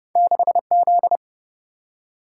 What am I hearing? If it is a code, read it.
Morse code, heard as a keyed tone, about 30 wpm, 700 Hz. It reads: 6Z
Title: Single-Letter Single-Number 30wpm